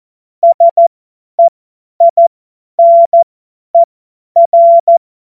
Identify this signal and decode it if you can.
Morse code, heard as a keyed tone, about 14 wpm, spaced as for 10 wpm, 680 Hz. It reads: SEINER
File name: Morsekurs / Witze / Witz 20. Dezember 2022 50 F70